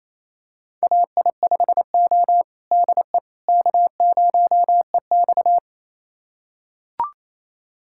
Morse code, heard as a keyed tone, about 28 wpm, 700 Hz. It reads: AI5O DE K0EX
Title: USA Callsign DE Callsign - 4 Characters - 28wpm